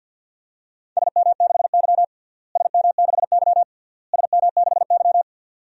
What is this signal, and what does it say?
Morse code, heard as a keyed tone, about 50 wpm, 700 Hz. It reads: SM6Y SM6Y SM6Y
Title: DXCC-100 Call Signs - 4 Characters - Repeated 3x - 50wpm